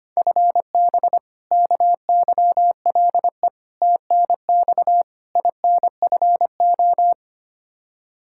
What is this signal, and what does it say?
Morse code, heard as a keyed tone, about 25 wpm, 700 Hz. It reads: FB KYLE TNX INFO
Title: Practice QSO - Exchange 3 - More Info - 25wpm